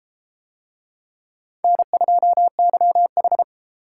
Morse code, heard as a keyed tone, about 30 wpm, 700 Hz. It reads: N2YH